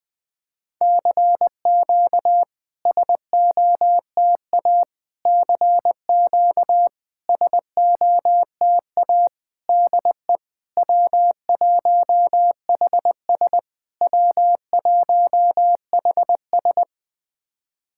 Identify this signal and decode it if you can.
Morse code, heard as a keyed tone, about 20 wpm, 690 Hz. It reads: CQ SOTA CQ SOTA DE W1HS W1HS
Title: SOTA QSO Practice 20wpm